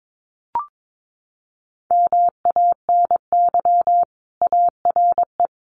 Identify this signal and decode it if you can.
Morse code, heard as a keyed tone, about 22 wpm, 700 Hz. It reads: MANY ARE
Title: Sentences from Top 1300 Words 22wpm